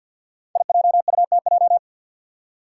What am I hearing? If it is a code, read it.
Morse code, heard as a keyed tone, about 52 wpm, 700 Hz. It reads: IJUTJ
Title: Five Challenging Letter-Number Combinations 50wpm